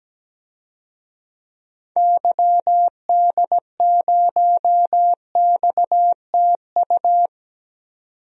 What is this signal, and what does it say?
Morse code, heard as a keyed tone, about 17 wpm, 700 Hz. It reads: YD0XTU
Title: DXCC-100 Call Signs - 6 Characters - 17wpm